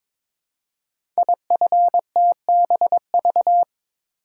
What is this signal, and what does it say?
Morse code, heard as a keyed tone, about 22 wpm, 700 Hz. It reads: IFTBV
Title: Code Groups - 5 Letters - 22wpm